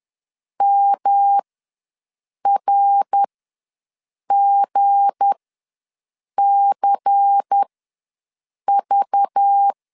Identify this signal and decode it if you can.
Morse code, heard as a keyed tone, about 10 wpm, spaced as for 6 wpm, 800 Hz. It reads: MRGCV
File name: LEZIONI / 8 YJ45 / 8 YJ ws 10wpm 4043